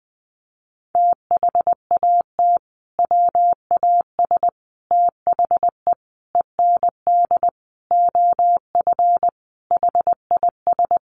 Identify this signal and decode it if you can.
Morse code, heard as a keyed tone, about 20 wpm, 700 Hz. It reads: THAT WAS THE END OF HIS
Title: Sentences from Top 1400 Words 20wpm